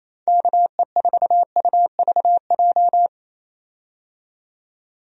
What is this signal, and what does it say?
Morse code, heard as a keyed tone, about 28 wpm, 700 Hz. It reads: KE4UVJ
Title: USA Call Signs - 6 Characters - 28wpm